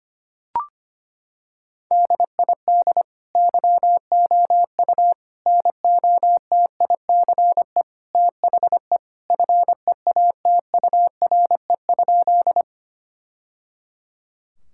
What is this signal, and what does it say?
Morse code, heard as a keyed tone, about 25 wpm, 700 Hz. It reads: DID YOU NOTICE THE FEATURE?
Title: Sentences from Top 1000 Words - Review Part 5 - 25wpm